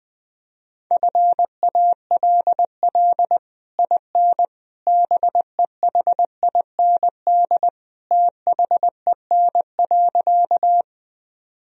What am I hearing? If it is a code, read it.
Morse code, heard as a keyed tone, about 21 wpm, 700 Hz. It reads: FALL IN BEHIND THEN.
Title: Sentences from Top 1000 Words - Review Part 2 - 20wpm